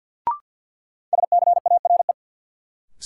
Morse code, heard as a keyed tone, about 47 wpm, 700 Hz. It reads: SCARE
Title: Sentences from Top 1000 Words - Review Part 5 - New Word Drill - 50wpm